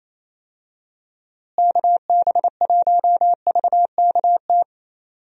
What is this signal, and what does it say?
Morse code, heard as a keyed tone, about 28 wpm, 700 Hz. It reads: KB1VKT
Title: USA Call Signs - 6 Characters - Repeated 3x - 28wpm